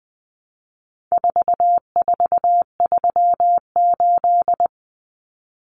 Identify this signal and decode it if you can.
Morse code, heard as a keyed tone, about 20 wpm, 700 Hz. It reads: 4438
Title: Numbers - 4-Digit 20wpm